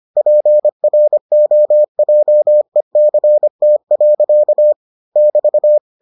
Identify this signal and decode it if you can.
Morse code, heard as a keyed tone, about 25 wpm, 600 Hz. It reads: PROJECT. <BT>